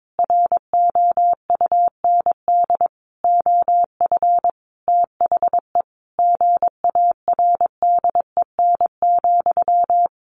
Morse code, heard as a keyed tone, about 22 wpm, 700 Hz. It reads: ROUND OF THE GARDEN,
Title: Sentences from Top 700 Words 22wpm